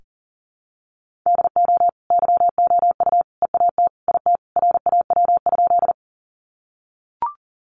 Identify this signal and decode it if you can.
Morse code, heard as a keyed tone, about 39 wpm, 700 Hz. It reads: DO YOU EAT IT RAW?
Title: Sentences from Top 1500 Words 40wpm